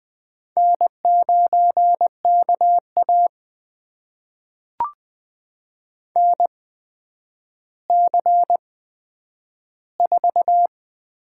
Morse code, heard as a keyed tone, about 20 wpm, spaced as for 10 wpm, 700 Hz. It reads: N9KA NC4
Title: USA Call Signs - 4 Characters - 8x Character Spacing - 20wpm